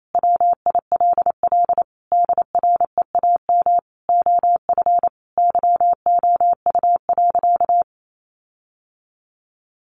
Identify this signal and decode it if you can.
Morse code, heard as a keyed tone, about 28 wpm, 700 Hz. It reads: WILL DREAM OF YOU.
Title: Sentences from Top 1500 Words 28wpm